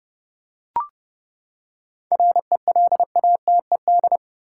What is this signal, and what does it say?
Morse code, heard as a keyed tone, about 30 wpm, 700 Hz. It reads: RELATED
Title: Seven Letter Words - 30wpm